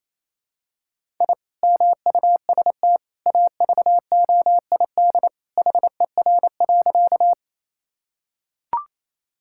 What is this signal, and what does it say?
Morse code, heard as a keyed tone, about 28 wpm, 700 Hz. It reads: I MUST AVOID HER.
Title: Sentences from Top 1000 Words - Review Part 6 - 28wpm